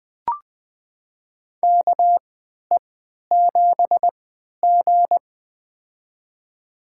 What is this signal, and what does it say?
Morse code, heard as a keyed tone, about 21 wpm, spaced as for 12 wpm, 700 Hz. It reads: KE7G
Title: USA Call Signs - 3x Character Spacing - 20wpm